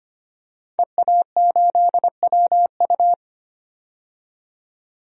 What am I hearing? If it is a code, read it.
Morse code, heard as a keyed tone, about 25 wpm, 700 Hz. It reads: EA8WU
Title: DXCC-100 Call Signs - 5 Characters - 25wpm